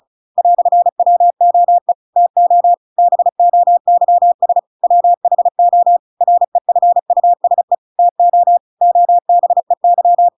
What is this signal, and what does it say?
Morse code, heard as a keyed tone, about 34 wpm, 700 Hz. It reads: "WOE TO BOYS WHO REFUSE TO OBEY